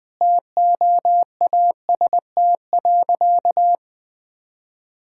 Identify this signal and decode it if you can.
Morse code, heard as a keyed tone, about 21 wpm, 700 Hz. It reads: TOAST.
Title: Easy Sentences with Hints - 20wpm